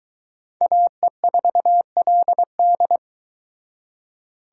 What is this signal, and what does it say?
Morse code, heard as a keyed tone, about 21 wpm, 700 Hz. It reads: AE4LD